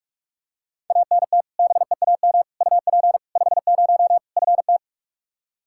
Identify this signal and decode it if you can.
Morse code, heard as a keyed tone, about 46 wpm, 700 Hz. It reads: ANT BEAM UP 50 FT
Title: Practice QSO - Exchange 2 - Info - 45wpm